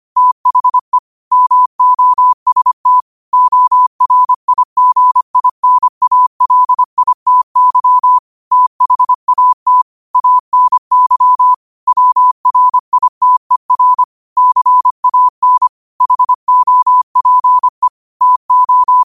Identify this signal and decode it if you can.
Morse code, heard as a keyed tone, about 25 wpm, 1000 Hz. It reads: THE MOST ORIGINALITY THAT ANY WRITER CAN HOPE TO
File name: qotdfiles / 25wpm